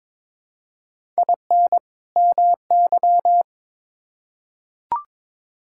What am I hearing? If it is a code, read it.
Morse code, heard as a keyed tone, about 22 wpm, 700 Hz. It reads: IN MY
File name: Build A Phrase - 2-Word Phrases - 22wpm